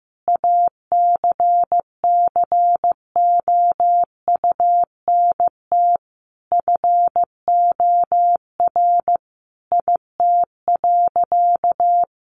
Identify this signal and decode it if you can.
Morse code, heard as a keyed tone, about 15 wpm, 700 Hz. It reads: ACCOUNT FOR IT.
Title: Sentences from Top 900 Words 15wpm